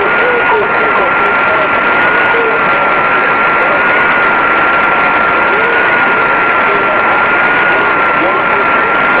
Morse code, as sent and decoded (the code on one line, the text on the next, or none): none